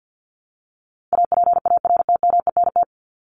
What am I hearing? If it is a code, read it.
Morse code, heard as a keyed tone, about 48 wpm, 700 Hz. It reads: APARTMENT
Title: Top 751-1000 Words 50wpm